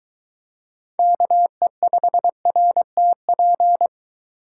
Morse code, heard as a keyed tone, about 22 wpm, 700 Hz. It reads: KE5RTP